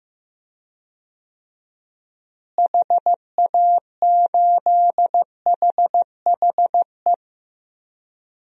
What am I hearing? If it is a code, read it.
Morse code, heard as a keyed tone, about 15 wpm, 700 Hz. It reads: HA8HHE
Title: DXCC-100 Call Signs - 6 Characters - Repeated 2x - 15wpm